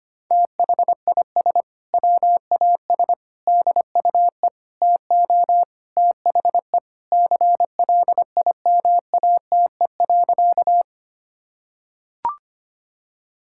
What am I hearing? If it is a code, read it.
Morse code, heard as a keyed tone, about 25 wpm, 700 Hz. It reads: THIS WAS DUE TO THE CLIMATE.
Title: Sentences from Top 1600 Words 25wpm